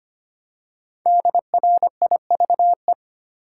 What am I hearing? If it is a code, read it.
Morse code, heard as a keyed tone, about 25 wpm, 700 Hz. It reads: DRIVE
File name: 5 Letter Words - First 3 Letters Given - 25wpm